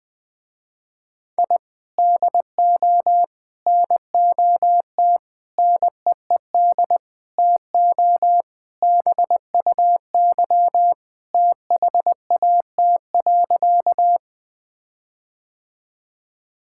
Morse code, heard as a keyed tone, about 20 wpm, 700 Hz. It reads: I DO NOT NEED TO BUY THAT.
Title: Sentences with 5-Word Phrases - Spoken Beginning - 20wpm